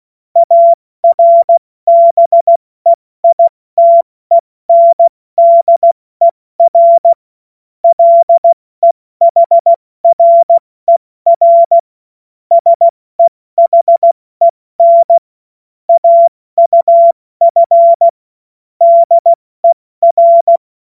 Morse code, heard as a keyed tone, about 16 wpm, 680 Hz. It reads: ARBEITENDER LEHRER SEHEN AUF DER
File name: Morsekurs / Witze / Witz 03. Januar 2023 70 F80